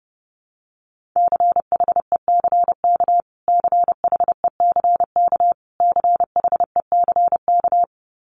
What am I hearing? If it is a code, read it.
Morse code, heard as a keyed tone, about 30 wpm, 700 Hz. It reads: CHECK CHECK CHECK
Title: Top 500 Words Repeated 30wpm